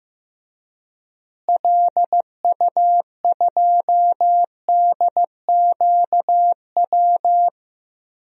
Morse code, heard as a keyed tone, about 15 wpm, 700 Hz. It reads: LU2DQW
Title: DXCC-100 Call Signs - 6 Characters - Repeated 3x - 15wpm